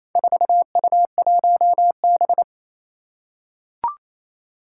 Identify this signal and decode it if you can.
Morse code, heard as a keyed tone, about 28 wpm, 700 Hz. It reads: 4U1B
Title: Four Challenging Letter-Number Combinations 28wpm